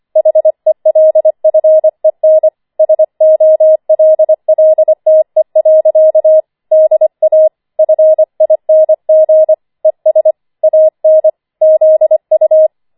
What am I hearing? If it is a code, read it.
Morse code, heard as a keyed tone, about 25 wpm, 630 Hz. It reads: HELFEN SOLLTE. DA FING ES AN ZU
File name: 25wpm